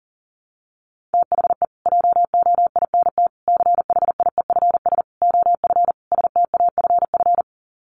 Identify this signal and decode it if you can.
Morse code, heard as a keyed tone, about 40 wpm, 700 Hz. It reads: THE JOINT CHIEFS OF STAFF
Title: Sets of 5 Words Encore2 40wpm